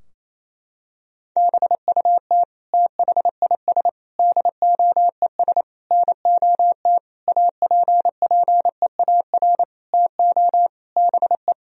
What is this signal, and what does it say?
Morse code, heard as a keyed tone, about 28 wpm, 700 Hz. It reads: BUT THIS DOES NOT APPEAR TO BE
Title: Sentences from Top 1000 Words - Review Part 2 - 28wpm